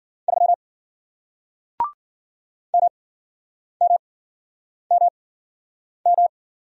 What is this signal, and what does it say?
Morse code, heard as a keyed tone, about 50 wpm, spaced as for 9 wpm, 700 Hz. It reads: 3 MMMM